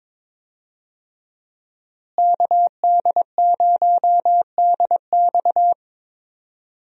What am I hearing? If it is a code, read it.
Morse code, heard as a keyed tone, about 22 wpm, 700 Hz. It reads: KD0DX
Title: USA Call Signs - 5 Characters - Repeated 2x - 22wpm